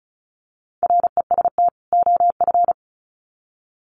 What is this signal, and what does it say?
Morse code, heard as a keyed tone, about 34 wpm, 700 Hz. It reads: REST OF
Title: Sets of 2 Words 35wpm